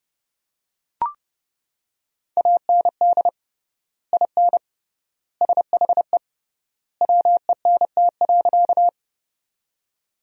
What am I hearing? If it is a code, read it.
Morse code, heard as a keyed tone, about 32 wpm, 700 Hz. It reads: AND IN SHE WENT.
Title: 3x Word-Spacing - Sentences from Top 200 Words - 30wpm